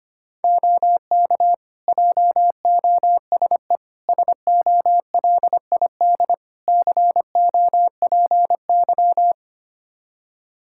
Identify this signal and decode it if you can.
Morse code, heard as a keyed tone, about 25 wpm, 700 Hz. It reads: OK JOSE SOLID COPY
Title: Practice QSO - Exchange 3 - More Info - 25wpm